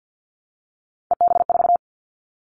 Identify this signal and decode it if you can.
Morse code, heard as a keyed tone, about 50 wpm, 700 Hz. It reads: EB4
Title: Callsign Prefixes for Top 25 DXCC - 50wpm